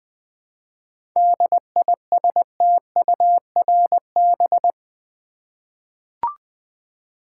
Morse code, heard as a keyed tone, about 20 wpm, 700 Hz. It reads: DISTURB
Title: 7 Letter Words - First 3 Letters Given - 20wpm